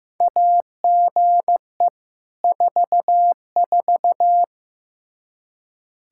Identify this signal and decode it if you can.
Morse code, heard as a keyed tone, about 15 wpm, 700 Hz. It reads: AGE 44
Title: Practice QSO - Exchange 3 - More Info - 15wpm